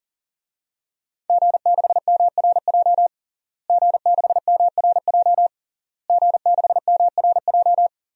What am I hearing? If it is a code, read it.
Morse code, heard as a keyed tone, about 37 wpm, 700 Hz. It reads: G6MRJ G6MRJ G6MRJ
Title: DXCC-100 Call Signs - 5 Characters - Repeated 3x - 40wpm